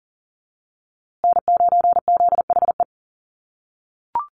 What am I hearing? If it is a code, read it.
Morse code, heard as a keyed tone, about 40 wpm, 700 Hz. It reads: N9ZHE
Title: USA Call Signs Repeated 2x 40wpm